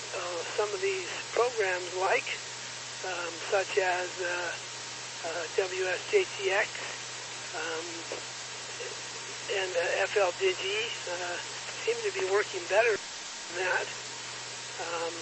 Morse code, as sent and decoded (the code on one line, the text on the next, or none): none